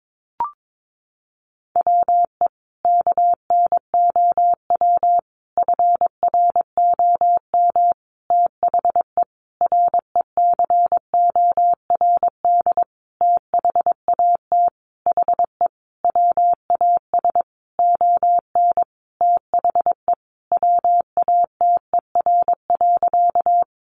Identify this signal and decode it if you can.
Morse code, heard as a keyed tone, about 22 wpm, 700 Hz. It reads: WE KNOW FROM THE RECORD THAT HE WAS ON THE WATER.
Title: Sentences from Top 500 Words 22wpm